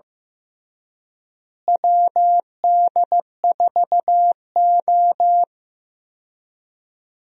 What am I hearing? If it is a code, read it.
Morse code, heard as a keyed tone, about 15 wpm, 700 Hz. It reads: WD4O
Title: USA Call Signs - 4 Characters - Repeated 3x - 15wpm